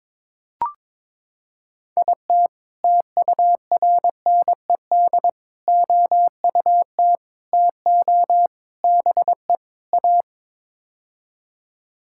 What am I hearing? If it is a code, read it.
Morse code, heard as a keyed tone, about 21 wpm, 700 Hz. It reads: IT TURNED OUT TO BE A
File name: practice / 22wpm / Sets of 6 Words - 22wpm